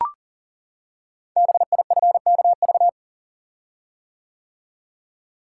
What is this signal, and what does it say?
Morse code, heard as a keyed tone, about 39 wpm, 690 Hz. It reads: BIFKV
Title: Code Groups - 5 Letter:Numbers - 40wpm